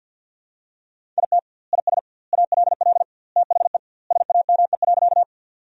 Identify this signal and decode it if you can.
Morse code, heard as a keyed tone, about 51 wpm, 700 Hz. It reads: IT IS ALL THE SAME.
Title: 2x Word-Spacing - Sentences from Top 200 Words - 50wpm